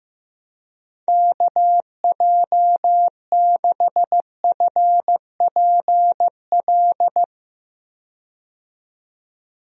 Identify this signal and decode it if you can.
Morse code, heard as a keyed tone, about 15 wpm, 700 Hz. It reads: KJ6FPL
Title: USA Call Signs - 6 Characters - 15wpm